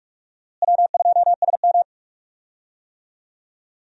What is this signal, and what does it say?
Morse code, heard as a keyed tone, about 44 wpm, 700 Hz. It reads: W2SM
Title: USA Call Signs - 4 Characters - Repeated 3x - 45wpm